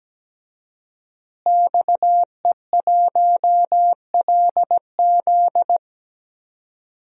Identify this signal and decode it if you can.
Morse code, heard as a keyed tone, about 17 wpm, 700 Hz. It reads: XE1LZ
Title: DXCC-100 Call Signs - 5 Characters - Repeated 2x - 17wpm